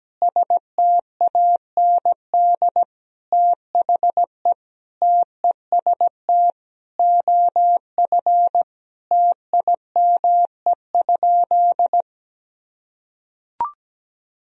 Morse code, heard as a keyed tone, about 17 wpm, 700 Hz. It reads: STAND THE TEST OF TIME?
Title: Sentences from Top 1000 Words - Review Part 3 - 17wpm